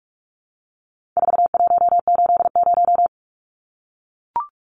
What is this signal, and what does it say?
Morse code, heard as a keyed tone, about 44 wpm, 700 Hz. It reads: 4180
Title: Numbers - 4-Digit 45wpm